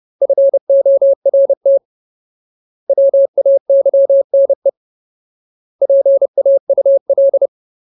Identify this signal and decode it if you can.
Morse code, heard as a keyed tone, about 30 wpm, 550 Hz. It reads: FORT WAYNE PAUL
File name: Ultimate Word List / 30WPM Ultimate